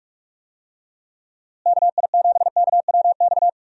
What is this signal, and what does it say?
Morse code, heard as a keyed tone, about 46 wpm, 700 Hz. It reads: KI7KWX
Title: USA Call Signs - 6 Characters - Repeated 3x - 45wpm